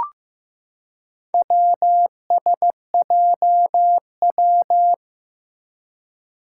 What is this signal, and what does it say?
Morse code, heard as a keyed tone, about 15 wpm, 700 Hz. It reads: WSJW
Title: Four Challenging Letter-Number Combinations 15wpm